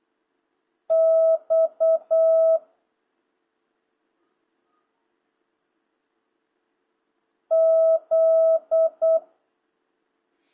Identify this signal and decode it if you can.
Morse code, heard as a keyed tone, about 8 wpm, 650 Hz. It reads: X Z